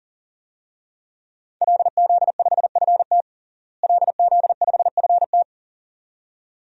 Morse code, heard as a keyed tone, about 40 wpm, 700 Hz. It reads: LZ5FT LZ5FT
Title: DXCC-100 Call Signs - 5 Characters - Repeated 2x - 40wpm